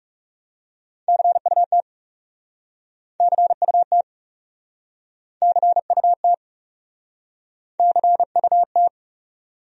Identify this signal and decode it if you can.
Morse code, heard as a keyed tone, about 42 wpm, 700 Hz. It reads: CUT CUT CUT CUT